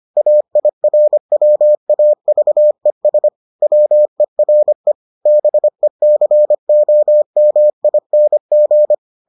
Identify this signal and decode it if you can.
Morse code, heard as a keyed tone, about 25 wpm, 600 Hz. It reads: AIRWAVES WERE BECOMING